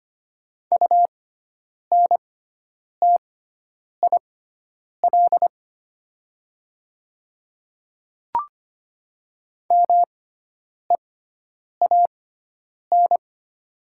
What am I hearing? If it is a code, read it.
Morse code, heard as a keyed tone, about 25 wpm, spaced as for 9 wpm, 700 Hz. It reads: UNTIL MEAN
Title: 6x Character Spacing - Top 300 Words - 25wpm